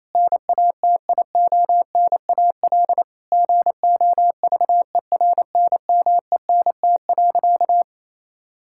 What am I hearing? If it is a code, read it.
Morse code, heard as a keyed tone, about 27 wpm, 700 Hz. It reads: NATIONAL GOVERNMENT.
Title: Sentences from Top 1000 Words - Review Part 6 - 28wpm